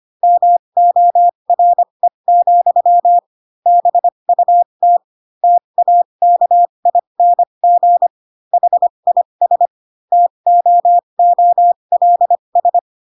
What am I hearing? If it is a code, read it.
Morse code, heard as a keyed tone, about 25 wpm, spaced as for 22 wpm, 700 Hz. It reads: MORE, BUT TAKING HIS TOOLS